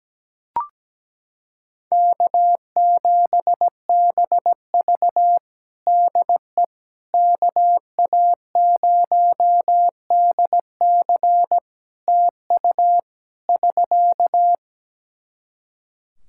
Morse code, heard as a keyed tone, about 17 wpm, 700 Hz. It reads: K7BV DE KA0DC TU <SK>
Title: Practice QSO - Exchange 4 - Ending - 17wpm